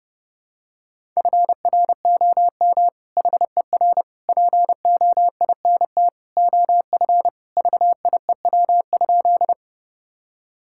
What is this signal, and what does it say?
Morse code, heard as a keyed tone, about 30 wpm, 700 Hz. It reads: FROM HER POINT OF VIEW?
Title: Sentences from Top 1000 Words - Review Part 6 - 30wpm